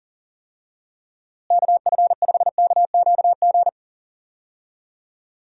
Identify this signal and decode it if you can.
Morse code, heard as a keyed tone, about 40 wpm, 700 Hz. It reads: KF5KQG
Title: USA Call Signs - 6 Characters - Repeated 2x - 40wpm